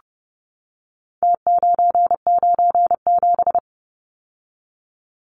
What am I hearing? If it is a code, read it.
Morse code, heard as a keyed tone, about 30 wpm, 700 Hz. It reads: T997
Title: Numbers - Contest Style - 4-Digit 30wpm